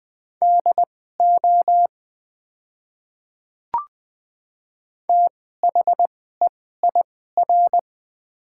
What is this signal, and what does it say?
Morse code, heard as a keyed tone, about 20 wpm, spaced as for 14 wpm, 700 Hz. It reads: DO THEIR
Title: 2x Character Spacing - Top 200 Words - 20wpm